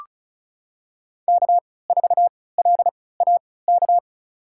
Morse code, heard as a keyed tone, about 34 wpm, spaced as for 20 wpm, 700 Hz. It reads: K4LAK
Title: USA Call Signs - 5 Characters - 3x Character Spacing - 35wpm